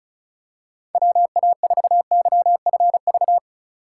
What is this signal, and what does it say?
Morse code, heard as a keyed tone, about 36 wpm, 700 Hz. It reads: WA4YFV
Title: USA Call Signs - 6 Characters - 2x Character Spacing - 35wpm